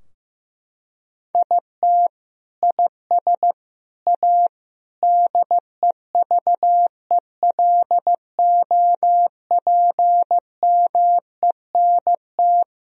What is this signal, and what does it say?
Morse code, heard as a keyed tone, about 15 wpm, 700 Hz. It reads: IT IS A DEVELOPMENT
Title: Sentences from Top 1000 Words - Review Part 4 - 15wpm